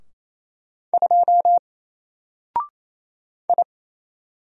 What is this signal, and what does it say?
Morse code, heard as a keyed tone, about 27 wpm, 700 Hz. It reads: 2 I
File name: Single-Letter-Number-28wpm